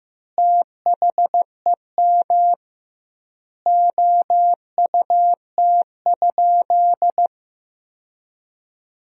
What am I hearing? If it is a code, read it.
Morse code, heard as a keyed tone, about 15 wpm, 700 Hz. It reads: THEM OUT?
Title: 2x Word-Spacing - Sentences from Top 300 Words - 15wpm